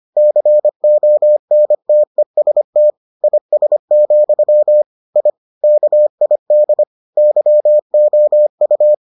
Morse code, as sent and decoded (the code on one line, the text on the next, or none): -.-. --- -. - . ... - / .. ... --..-- / .. / -.- .. -.. / -.-- --- ..-
CONTEST IS, I KID YOU